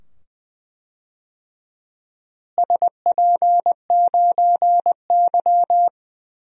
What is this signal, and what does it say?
Morse code, heard as a keyed tone, about 20 wpm, 700 Hz. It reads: SP9Y
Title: DXCC-100 Call Signs - 4 Characters - Repeated 2x - 20wpm